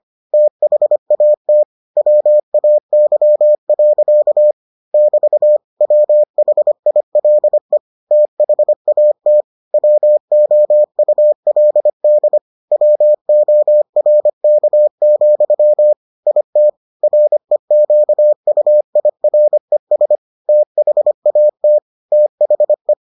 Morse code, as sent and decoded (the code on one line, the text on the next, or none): - .... .- - / .-- .- -.-- .-.-.- / -...- / .-- .... .. .-.. / - .... .- - / .-- --- ..- .-.. -.. / .-- --- .-. -.- --..-- / .. - / .-. --.- ..- .. .-. ... / - .... .- - / - .... .
THAT WAY. <BT> WHILE THAT WOULD WORK, IT REQUIRES THAT THE